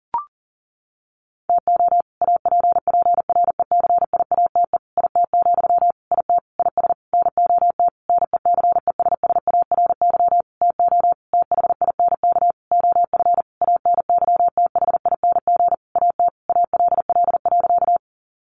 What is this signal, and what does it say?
Morse code, heard as a keyed tone, about 40 wpm, 700 Hz. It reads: TO APPRECIATE IT, IT IS NOT NECESSARY TO THINK OF ANYTHING AT ALL.
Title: Sentences from Top 1300 Words 40wpm